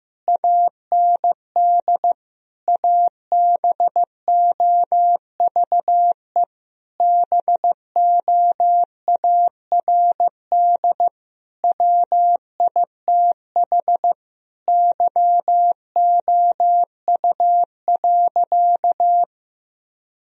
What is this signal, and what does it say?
Morse code, heard as a keyed tone, about 15 wpm, 700 Hz. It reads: AND ABOVE BOARD WITH YOU.
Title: Sentences from Top 1000 Words - Review Part 4 - 15wpm